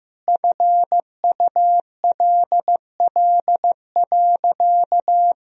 Morse code, heard as a keyed tone, about 15 wpm, 700 Hz. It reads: FULL.
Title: Sentences from Top 1000 Words - Review Part 3 - 15wpm